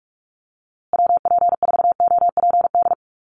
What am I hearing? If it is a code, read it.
Morse code, heard as a keyed tone, about 46 wpm, 700 Hz. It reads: WP4OPD
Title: USA Call Signs Repeated 2x 45wpm